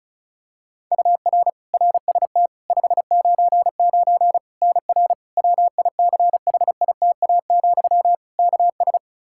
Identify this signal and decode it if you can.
Morse code, heard as a keyed tone, about 36 wpm, 700 Hz. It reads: UR RST 599 NR WICHITA, KS
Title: K1USN SST QSO Practice 35wpm